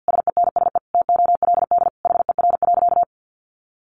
Morse code, heard as a keyed tone, about 50 wpm, 700 Hz. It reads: SENSE TOLD HER.
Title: Sentences from Top 1200 Words 50wpm